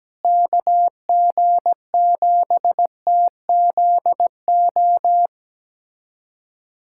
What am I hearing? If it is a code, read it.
Morse code, heard as a keyed tone, about 17 wpm, 700 Hz. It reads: KG7TZO
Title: USA Callsign DE Callsign - 6 Characters - 17wpm